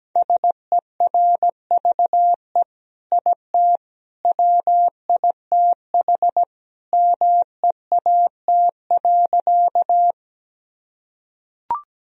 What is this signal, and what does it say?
Morse code, heard as a keyed tone, about 17 wpm, 700 Hz. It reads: SERVE IT WITH MEAT.
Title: Sentences from Top 1000 Words - Review Part 3 - 17wpm